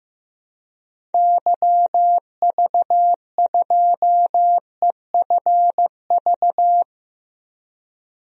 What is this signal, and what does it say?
Morse code, heard as a keyed tone, about 15 wpm, 700 Hz. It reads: YV2EFV
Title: DXCC-100 Call Signs - 6 Characters - Repeated 2x - 15wpm